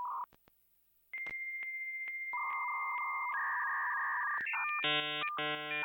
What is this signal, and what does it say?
Dial-up modem handshake.